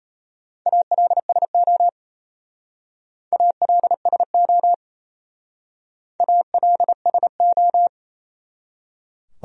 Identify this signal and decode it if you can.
Morse code, heard as a keyed tone, about 32 wpm, 700 Hz. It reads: ALSO ALSO ALSO
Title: Speed-Racing Top 300 Words 28wpm